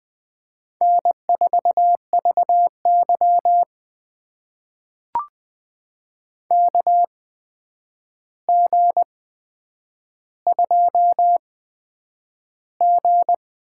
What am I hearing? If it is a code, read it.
Morse code, heard as a keyed tone, about 21 wpm, spaced as for 9 wpm, 700 Hz. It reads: N4VY KG2G
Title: USA Call Signs - 4 Characters - 8x Character Spacing - 20wpm